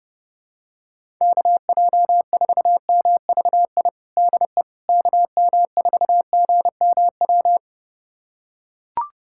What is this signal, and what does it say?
Morse code, heard as a keyed tone, about 32 wpm, 700 Hz. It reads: KJ4MVI DE KM4GMW
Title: USA Callsign DE Callsign - 6 Characters - 30wpm